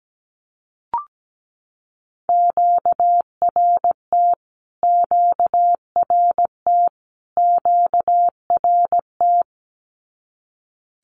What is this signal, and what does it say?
Morse code, heard as a keyed tone, about 17 wpm, 700 Hz. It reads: QRT QRT QRT
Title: Q-Codes Repeated 3x 17wpm